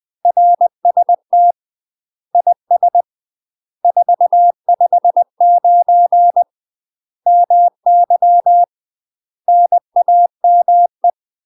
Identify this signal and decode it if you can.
Morse code, heard as a keyed tone, about 20 wpm, 700 Hz. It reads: RST IS 459 MY NAME
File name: UK MASTER QSOs TRAINING / 20WPM / UK MASTER QSOs 0057